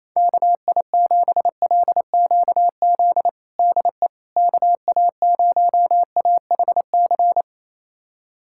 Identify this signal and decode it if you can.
Morse code, heard as a keyed tone, about 28 wpm, 700 Hz. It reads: KI7LQZ DE KA0AHC
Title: USA Callsign DE Callsign 28wpm